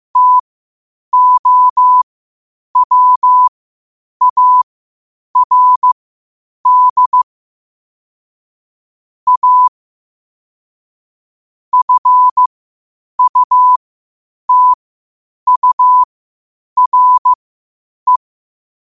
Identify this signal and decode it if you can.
Morse code, heard as a keyed tone, about 15 wpm, spaced as for 8 wpm, 1000 Hz. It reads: TOWARD A FUTURE
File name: qotdfiles / 8wpm